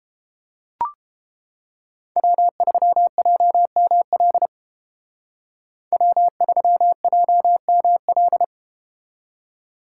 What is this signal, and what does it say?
Morse code, heard as a keyed tone, about 30 wpm, 700 Hz. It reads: W3JML W3JML